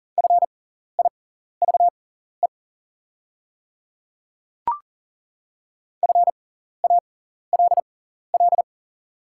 Morse code, heard as a keyed tone, about 40 wpm, spaced as for 14 wpm, 700 Hz. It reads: FIVE FALL
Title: 6x Character Spacing - Top 500 Words - 40wpm